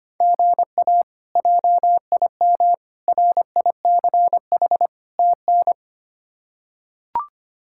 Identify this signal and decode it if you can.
Morse code, heard as a keyed tone, about 25 wpm, 700 Hz. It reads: GA JIM RICH TN
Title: K1USN SST QSO Practice - 25wpm